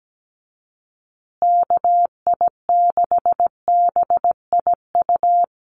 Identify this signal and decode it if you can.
Morse code, heard as a keyed tone, about 17 wpm, 700 Hz. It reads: KI6BIU
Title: USA Call Signs Repeated 3x 17wpm